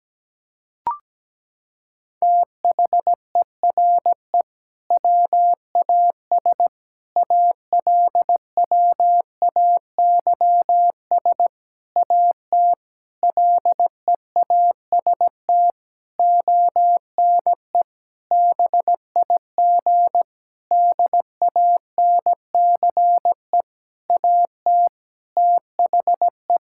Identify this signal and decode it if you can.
Morse code, heard as a keyed tone, about 17 wpm, 700 Hz. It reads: THERE WAS ALWAYS AT LEAST ONE BIG DANCE AT THE
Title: Sentences from Top 1600 Words 17wpm